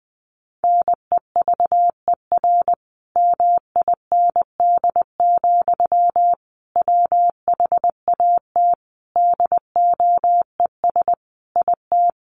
Morse code, heard as a keyed tone, about 20 wpm, 700 Hz. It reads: NEVER MIND, WHAT DOES IT
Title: Sentences from Top 700 Words 20wpm